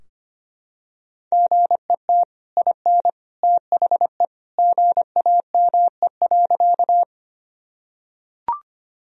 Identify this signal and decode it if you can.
Morse code, heard as a keyed tone, about 25 wpm, 700 Hz. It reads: GET IN THE GAME.
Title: Sentences from Top 1000 Words - Review Part 3 - 25wpm